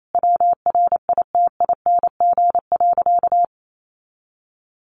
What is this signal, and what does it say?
Morse code, heard as a keyed tone, about 28 wpm, 700 Hz. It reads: WRITING.
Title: Sentences from Top 900 Words 28wpm